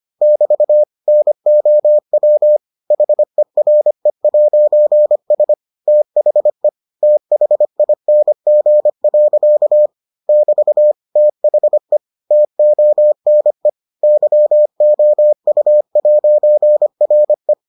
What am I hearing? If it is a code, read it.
Morse code, heard as a keyed tone, about 25 wpm, 600 Hz. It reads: <BT> NOW HERE'S THE THING. <BT> THE TONE YOU'RE